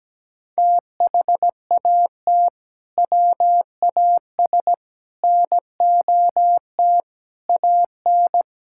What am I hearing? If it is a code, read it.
Morse code, heard as a keyed tone, about 17 wpm, 700 Hz. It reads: THAT WAS NOT AN
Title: Sentences from Top 1000 Words - Review Part 5 - 17wpm